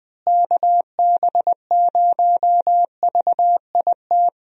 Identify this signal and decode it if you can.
Morse code, heard as a keyed tone, about 20 wpm, 700 Hz. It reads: KB0VIT
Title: USA Call Signs - 6 Characters - Repeated 3x - 20wpm